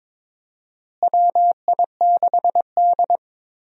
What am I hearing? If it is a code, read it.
Morse code, heard as a keyed tone, about 22 wpm, 700 Hz. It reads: WI6D
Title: USA Call Signs - 4 Characters - Repeated 3x - 22wpm